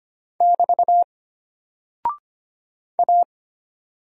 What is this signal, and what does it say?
Morse code, heard as a keyed tone, about 25 wpm, 700 Hz. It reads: <BT> A
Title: Single Character (Letter:Number:Punctuation:Prosign) Alt 25wpm